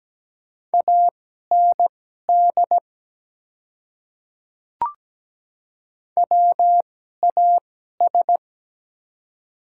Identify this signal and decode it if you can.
Morse code, heard as a keyed tone, about 17 wpm, spaced as for 12 wpm, 700 Hz. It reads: AND WAS
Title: 2x Character Spacing - Top 100 Words - 17wpm